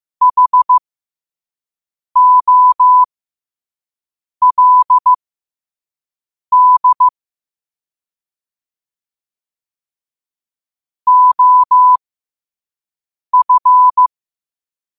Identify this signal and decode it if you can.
Morse code, heard as a keyed tone, about 15 wpm, spaced as for 5 wpm, 1000 Hz. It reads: HOLD OF
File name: qotdfiles / 5wpm